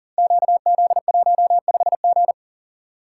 Morse code, heard as a keyed tone, about 40 wpm, 700 Hz. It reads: QZ15G
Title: Code Groups - 5 Letter:Numbers - 40wpm